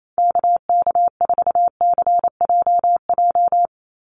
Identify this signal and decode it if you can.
Morse code, heard as a keyed tone, about 28 wpm, 700 Hz. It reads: KK4CJJ
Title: USA Call Signs 28wpm